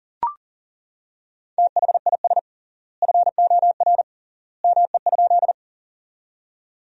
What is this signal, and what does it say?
Morse code, heard as a keyed tone, about 40 wpm, 700 Hz. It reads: THIS FOR ME?
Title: 3x Word-Spacing - Sentences from Top 200 Words - 40wpm